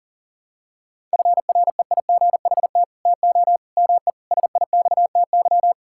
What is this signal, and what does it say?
Morse code, heard as a keyed tone, about 40 wpm, 700 Hz. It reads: FREIGHT TO ME SIXTY
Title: Pangram Challenge v2 - 40wpm